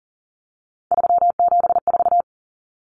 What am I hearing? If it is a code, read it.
Morse code, heard as a keyed tone, about 40 wpm, 700 Hz. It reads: 374
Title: Numbers - 3-Digit 40wpm